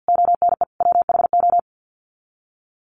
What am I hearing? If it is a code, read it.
Morse code, heard as a keyed tone, about 50 wpm, 700 Hz. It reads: ONE WHO
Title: Sets of 5 Words Encore2 50wpm